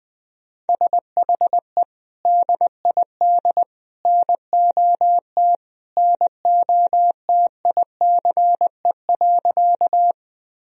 Morse code, heard as a keyed tone, about 20 wpm, 700 Hz. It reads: SHE DID NOT NOTICE.
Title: Sentences from Top 1000 Words - Review Part 2 - 20wpm